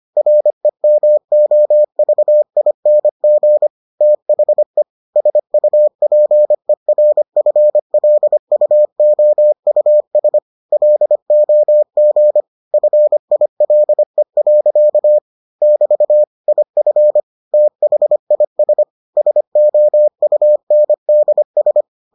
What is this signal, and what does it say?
Morse code, heard as a keyed tone, about 25 wpm, 600 Hz. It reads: REMOVING THE SUPERFLUOUS LOG FILE. <BT> IF THIS SOUNDS